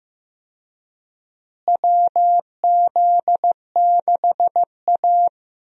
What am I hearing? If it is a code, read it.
Morse code, heard as a keyed tone, about 16 wpm, 700 Hz. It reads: WZ6A
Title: USA Call Signs - 4 Characters - Repeated 2x - 15wpm